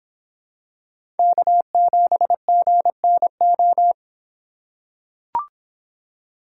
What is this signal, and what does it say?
Morse code, heard as a keyed tone, about 26 wpm, 700 Hz. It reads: K7GNO